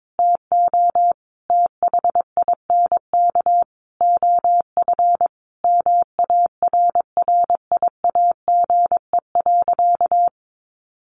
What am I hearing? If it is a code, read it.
Morse code, heard as a keyed tone, about 22 wpm, 700 Hz. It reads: TO THINK OF MARRIAGE.
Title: Sentences from Top 800 Words 22wpm